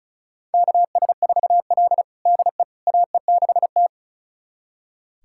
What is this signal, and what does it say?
Morse code, heard as a keyed tone, about 36 wpm, 700 Hz. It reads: KS4L DE AE6T
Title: USA Callsign DE Callsign - 4 Characters - 35wpm